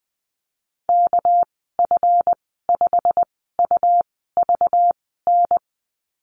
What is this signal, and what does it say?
Morse code, heard as a keyed tone, about 20 wpm, spaced as for 14 wpm, 700 Hz. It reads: KF5UVN
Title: 245 Lesson 39g - Letter z - Call Signs 20wpm